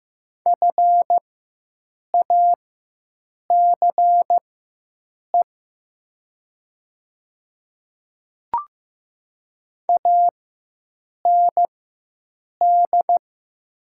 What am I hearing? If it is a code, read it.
Morse code, heard as a keyed tone, about 15 wpm, spaced as for 7 wpm, 700 Hz. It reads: FACE AND
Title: 4x Character Spacing - Top 300 Words - 15wpm